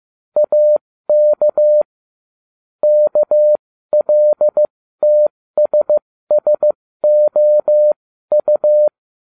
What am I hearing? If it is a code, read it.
Morse code, heard as a keyed tone, about 15 wpm, spaced as for 13 wpm, 600 Hz. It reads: AK KLTSSOU